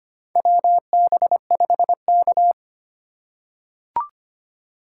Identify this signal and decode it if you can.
Morse code, heard as a keyed tone, about 25 wpm, 700 Hz. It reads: WB5K
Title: K1USN SST QSO Practice - 25wpm